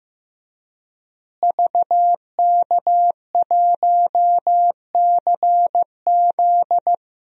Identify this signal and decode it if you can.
Morse code, heard as a keyed tone, about 15 wpm, 700 Hz. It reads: VK1CZ
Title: DXCC-100 Call Signs - 5 Characters - Repeated 3x - 15wpm